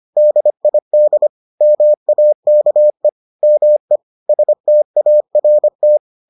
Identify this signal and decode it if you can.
Morse code, heard as a keyed tone, about 25 wpm, 600 Hz. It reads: DID MAKE ME START